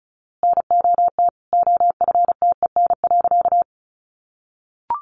Morse code, heard as a keyed tone, about 34 wpm, 700 Hz. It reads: NOT OFTEN.
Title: A Sound Of Thunder 35wpm